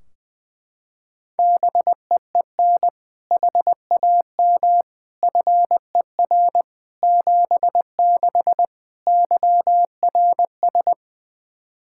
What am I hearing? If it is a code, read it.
Morse code, heard as a keyed tone, about 20 wpm, 700 Hz. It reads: BEEN HAM FER 76 YRS
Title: K1USN SST QSO Practice 20wpm